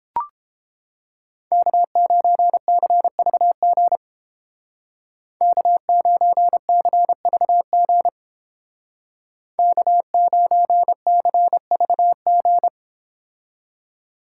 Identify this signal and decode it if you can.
Morse code, heard as a keyed tone, about 30 wpm, 700 Hz. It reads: K9CVG K9CVG K9CVG